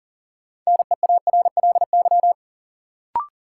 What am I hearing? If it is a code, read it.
Morse code, heard as a keyed tone, about 40 wpm, 700 Hz. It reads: NEARLY
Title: Sentences from Top 1300 Words - New Word Drill - 40wpm